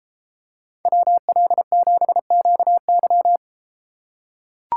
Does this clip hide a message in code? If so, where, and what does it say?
Yes: Morse code from 0.9 to 3.4 s, heard as a keyed tone, about 32 wpm, 700 Hz. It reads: WL7QY